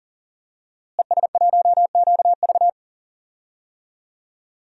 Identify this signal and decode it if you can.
Morse code, heard as a keyed tone, about 40 wpm, 700 Hz. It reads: ES1QV
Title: DXCC-100 Call Signs - 5 Characters - Repeated 3x - 40wpm